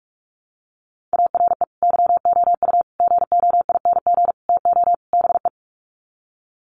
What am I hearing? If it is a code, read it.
Morse code, heard as a keyed tone, about 44 wpm, 700 Hz. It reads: ARE YOU GOING TO BE
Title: Sets of 5 Words Encore 45wpm